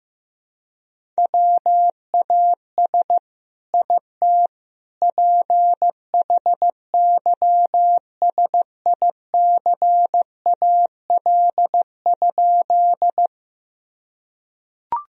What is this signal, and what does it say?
Morse code, heard as a keyed tone, about 15 wpm, 700 Hz. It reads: WAS IT PHYSICAL?
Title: Sentences from Top 1000 Words - Review Part 4 - 15wpm